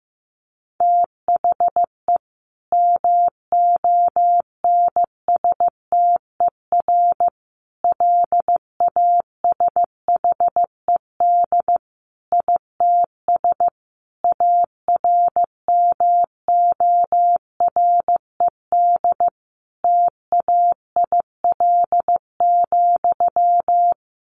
Morse code, heard as a keyed tone, about 15 wpm, 700 Hz. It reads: THE MONSTER LASHED ITS ARMORED TAIL,
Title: A Sound Of Thunder 15wpm